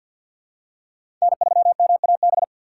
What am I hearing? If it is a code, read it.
Morse code, heard as a keyed tone, about 47 wpm, 700 Hz. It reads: N3MAB
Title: USA Call Signs - 4x Character Spacing - 50wpm